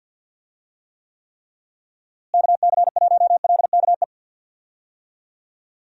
Morse code, heard as a keyed tone, about 52 wpm, 700 Hz. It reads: KC1LKE
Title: USA Call Signs - 6 Characters - Repeated 3x - 50wpm